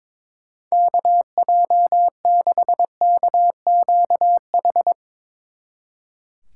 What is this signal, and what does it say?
Morse code, heard as a keyed tone, about 22 wpm, 700 Hz. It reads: KJ6KQH
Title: Speed Racing - USA 2x3 Call Signs 22wpm